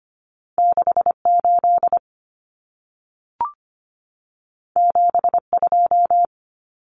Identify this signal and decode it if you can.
Morse code, heard as a keyed tone, about 25 wpm, 700 Hz. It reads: 68 72
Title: Common Cut Numbers - 2-Digit 25wpm